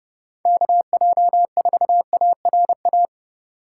Morse code, heard as a keyed tone, about 35 wpm, 700 Hz. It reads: KJ4ARA